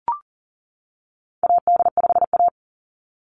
Morse code, heard as a keyed tone, about 40 wpm, 700 Hz. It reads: AD5A
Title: NA SOTA Activator Callsigns 40wpm